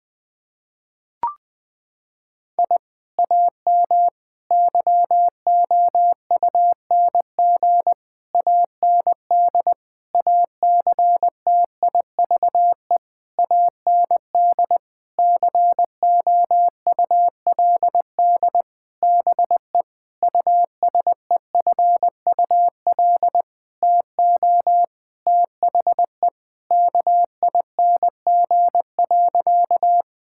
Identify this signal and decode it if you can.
Morse code, heard as a keyed tone, about 20 wpm, 700 Hz. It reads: I AM YOUNG AND ACTIVE AND COULD BE USEFUL TO THE KING.
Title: Sentences from Top 1000 Words - Review Part 6 - 20wpm